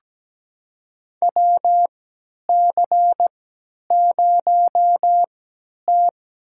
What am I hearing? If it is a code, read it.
Morse code, heard as a keyed tone, about 17 wpm, spaced as for 10 wpm, 700 Hz. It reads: WC0T
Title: USA Call Signs - 4 Characters - 3x Character Spacing - 17wpm